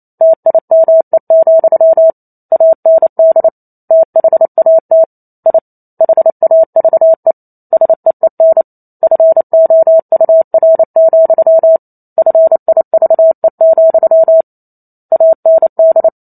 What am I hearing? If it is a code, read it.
Morse code, heard as a keyed tone, about 28 wpm, 650 Hz. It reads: TIME, AND THAT I HAVE SEEN FOUR, FIVE, AND
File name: Chapter 2 - 28 WPM